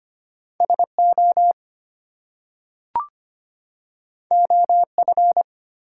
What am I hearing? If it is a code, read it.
Morse code, heard as a keyed tone, about 25 wpm, 700 Hz. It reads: SO OF
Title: Two Letter Words - 25wpm